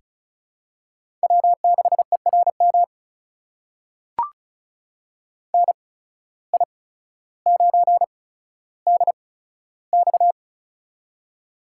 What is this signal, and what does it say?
Morse code, heard as a keyed tone, about 36 wpm, spaced as for 15 wpm, 700 Hz. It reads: W6ERM NI9DX
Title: USA Call Signs - 8x Character Spacing - 35wpm